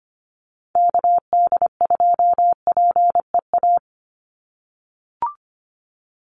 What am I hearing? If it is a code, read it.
Morse code, heard as a keyed tone, about 25 wpm, 700 Hz. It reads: KD2PEA
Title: USA Call Signs 25wpm